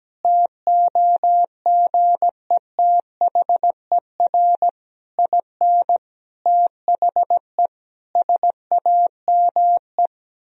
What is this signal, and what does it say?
Morse code, heard as a keyed tone, about 17 wpm, 700 Hz. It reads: TOGETHER IN THE SAME
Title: Sentences from Top 1000 Words - Review Part 3 - 17wpm